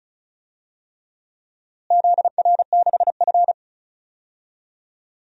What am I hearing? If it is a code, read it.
Morse code, heard as a keyed tone, about 36 wpm, 700 Hz. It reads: ZR6F
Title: DXCC-100 Call Signs - 4 Characters - Repeated 3x - 35wpm